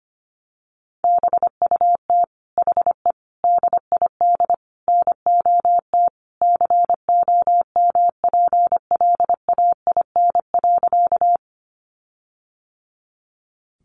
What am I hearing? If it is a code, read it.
Morse code, heard as a keyed tone, about 25 wpm, 700 Hz. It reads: BUT HE DID NOT COMPLAIN.
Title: Sentences from Top 1300 Words 25wpm